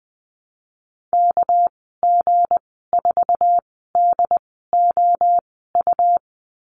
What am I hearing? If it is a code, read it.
Morse code, heard as a keyed tone, about 20 wpm, spaced as for 15 wpm, 700 Hz. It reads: KG4DOU
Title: 213 Lesson 34g - Number 8 - Call Signs 20wpm